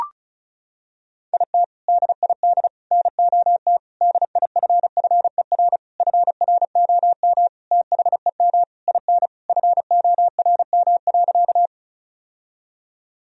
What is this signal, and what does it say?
Morse code, heard as a keyed tone, about 34 wpm, 700 Hz. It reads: IT DID NOT DIFFER FROM THEM IN FORM.
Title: Sentences from Top 1000 Words - Review Part 1 - 35wpm